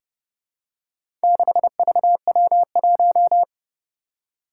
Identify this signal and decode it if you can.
Morse code, heard as a keyed tone, about 32 wpm, 700 Hz. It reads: 6VW1
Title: Four Challenging Letter-Number Combinations 30wpm